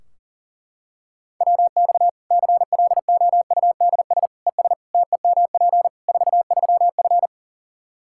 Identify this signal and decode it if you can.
Morse code, heard as a keyed tone, about 40 wpm, 700 Hz. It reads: WX CLOUDS ES TEMP 43F
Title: Practice QSO - Exchange 2 - Info - 40wpm